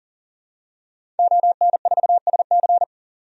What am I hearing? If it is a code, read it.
Morse code, heard as a keyed tone, about 38 wpm, 700 Hz. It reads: ON4SC
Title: DXCC-100 Call Signs - 5 Characters - 40wpm